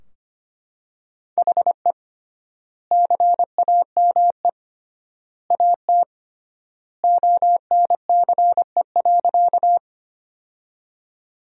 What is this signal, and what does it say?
Morse code, heard as a keyed tone, about 25 wpm, 700 Hz. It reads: HE CAME AT ONCE.
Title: 3x Word-Spacing - Sentences from Top 300 Words - 25wpm